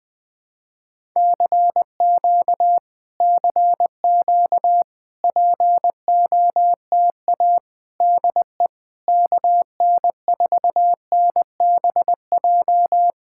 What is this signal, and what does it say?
Morse code, heard as a keyed tone, about 20 wpm, 700 Hz. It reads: CQ CQ POTA DE KN4NBJ
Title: POTA QSO Practice - 20wpm